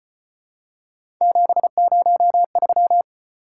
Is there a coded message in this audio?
Yes: Morse code from 1.2 to 3.0 s, heard as a keyed tone, about 34 wpm, 700 Hz. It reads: 703